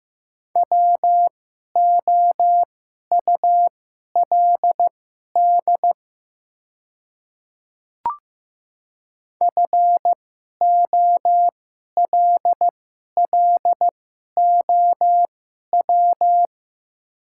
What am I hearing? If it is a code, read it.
Morse code, heard as a keyed tone, about 15 wpm, spaced as for 11 wpm, 700 Hz. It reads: WOULD FOLLOW
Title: 2x Character Spacing - Top 200 Words - 15wpm